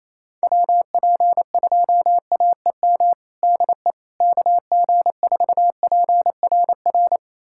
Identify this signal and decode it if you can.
Morse code, heard as a keyed tone, about 28 wpm, 700 Hz. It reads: WP2AEM DE KG4PRR
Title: USA Callsign DE Callsign - 6 Characters - 28wpm